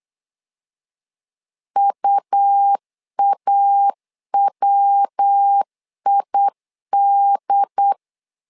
Morse code, heard as a keyed tone, about 8 wpm, 800 Hz. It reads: UAWID
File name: LEZIONI / 3 MWRO / 3 MW 8wpm 2501